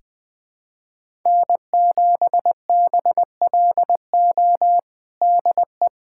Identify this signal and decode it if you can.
Morse code, heard as a keyed tone, about 21 wpm, 700 Hz. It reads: N7BLO DE
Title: USA Callsign DE Callsign - 5 Characters - 20wpm